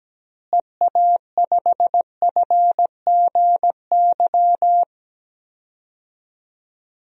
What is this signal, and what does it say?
Morse code, heard as a keyed tone, about 17 wpm, 700 Hz. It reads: EA5FGY
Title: DXCC-100 Call Signs - 6 Characters - Repeated 2x - 17wpm